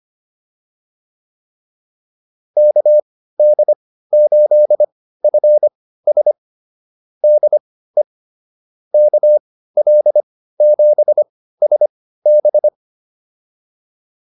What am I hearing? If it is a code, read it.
Morse code, heard as a keyed tone, about 25 wpm, spaced as for 15 wpm, 600 Hz. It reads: KD8FS DE KL7SB
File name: QSO107 15